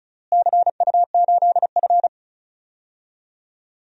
Morse code, heard as a keyed tone, about 36 wpm, 700 Hz. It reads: CU8F
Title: DXCC-100 Call Signs - 4 Characters - Repeated 2x - 35wpm